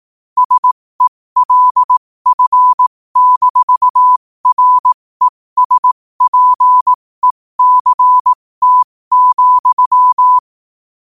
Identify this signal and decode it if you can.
Morse code, heard as a keyed tone, about 18 wpm, spaced as for 15 wpm, 1000 Hz. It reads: SELF-RESPECT,
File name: qotdfiles / 15wpm